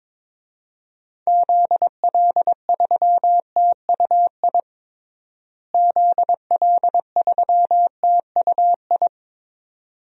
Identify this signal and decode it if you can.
Morse code, heard as a keyed tone, about 22 wpm, 700 Hz. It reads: ZL3TUI ZL3TUI
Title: DXCC-100 Call Signs - 6 Characters - Repeated 2x - 22wpm